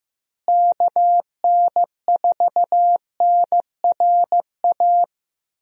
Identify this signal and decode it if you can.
Morse code, heard as a keyed tone, about 15 wpm, 700 Hz. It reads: KN4NRA
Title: USA Call Signs - 6 Characters - Repeated 3x - 15wpm